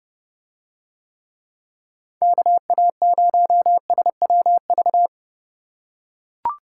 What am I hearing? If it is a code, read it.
Morse code, heard as a keyed tone, about 30 wpm, 700 Hz. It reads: KA0SWV